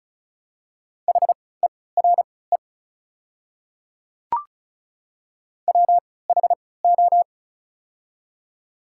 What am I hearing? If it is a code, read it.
Morse code, heard as a keyed tone, about 36 wpm, spaced as for 20 wpm, 690 Hz. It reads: HERE WHO
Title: 3x Character Spacing - Top 200 Words - 35wpm